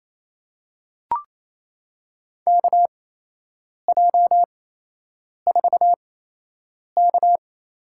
Morse code, heard as a keyed tone, about 28 wpm, spaced as for 8 wpm, 700 Hz. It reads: KJ4K
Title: USA Call Signs - 6 Characters - 8x Character Spacing - 28wpm